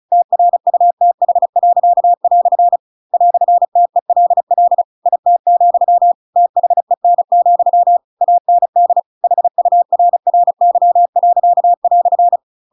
Morse code, heard as a keyed tone, about 34 wpm, 700 Hz. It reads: TRUTH." "TELL IT, THEN, AND HURRY."